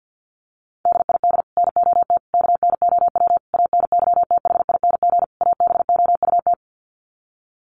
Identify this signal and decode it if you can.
Morse code, heard as a keyed tone, about 50 wpm, 700 Hz. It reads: DID NOT KNOW ANYTHING ABOUT
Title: Sets of 5 Words Encore2 50wpm